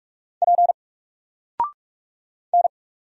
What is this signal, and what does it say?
Morse code, heard as a keyed tone, about 44 wpm, 700 Hz. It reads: P N
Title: Single Character (Letter:Number:Punctuation:Prosign) Alt 45wpm